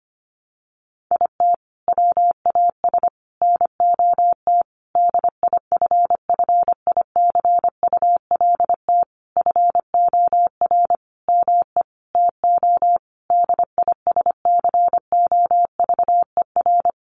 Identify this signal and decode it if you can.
Morse code, heard as a keyed tone, about 25 wpm, 700 Hz. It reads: IT WAS NOT DIFFICULT FOR ME TO DISCOVER
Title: Sentences from Top 1200 Words 25wpm